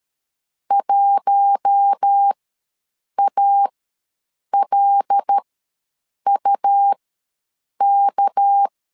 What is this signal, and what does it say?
Morse code, heard as a keyed tone, about 13 wpm, spaced as for 7 wpm, 800 Hz. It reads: 1ALUK
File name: LEZIONI / 6 PKX0 / 6 PK ws 12wpm 3116